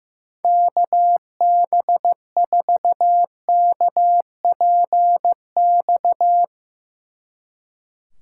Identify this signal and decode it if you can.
Morse code, heard as a keyed tone, about 16 wpm, 700 Hz. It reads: KB4KPX
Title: Speed Racing - USA 2x3 Call Signs 15wpm